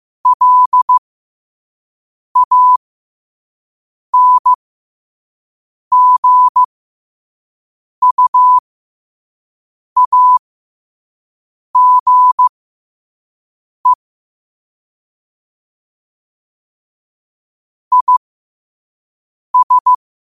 Morse code, heard as a keyed tone, about 15 wpm, spaced as for 5 wpm, 1000 Hz. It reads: LANGUAGE IS
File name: qotdfiles / 5wpm